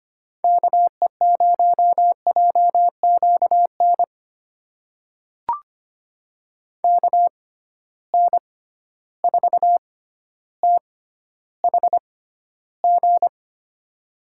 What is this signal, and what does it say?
Morse code, heard as a keyed tone, about 25 wpm, spaced as for 13 wpm, 700 Hz. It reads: KE0JQN KN4THG
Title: USA Call Signs - 6 Characters - 6x Character Spacing - 25wpm